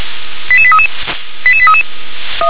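Five-tone selective call (ZVEI), heard as quick runs of five tones, tones of about 70 ms. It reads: 88833 88833
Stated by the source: Received via SDR; ZVEI five-tone call 88173.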